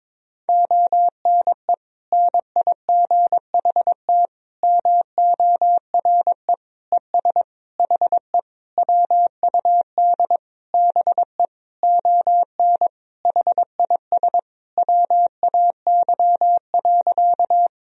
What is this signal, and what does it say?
Morse code, heard as a keyed tone, about 21 wpm, 700 Hz. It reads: ONE NIGHT MORE ES HE WUD BE ON HIS WAY.
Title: Sentences from Top 300 Words and Abbreviations 22wpm